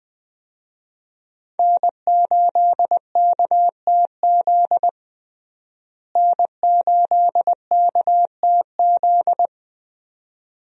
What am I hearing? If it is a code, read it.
Morse code, heard as a keyed tone, about 20 wpm, 700 Hz. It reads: N8KTZ N8KTZ
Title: USA Call Signs - 5 Characters - Repeated 2x - 20wpm